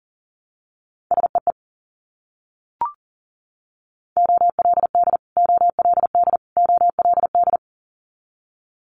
Morse code, heard as a keyed tone, about 40 wpm, 700 Hz. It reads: SEE OLD OLD OLD
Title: Multi-letter Phonemes Repeated 3x 40wpm